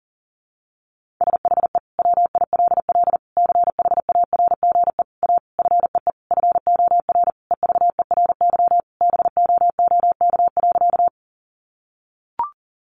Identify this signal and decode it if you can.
Morse code, heard as a keyed tone, about 40 wpm, 700 Hz. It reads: SHE WILL CHARGE A FEE FOR EVERY BOOK.
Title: Sentences from Top 1500 Words 40wpm